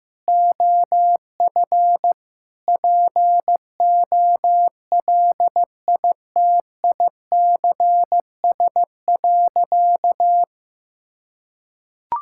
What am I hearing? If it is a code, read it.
Morse code, heard as a keyed tone, about 15 wpm, 700 Hz. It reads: OF POLITICS.
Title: Sentences from Top 1000 Words - Review Part 6 - 15wpm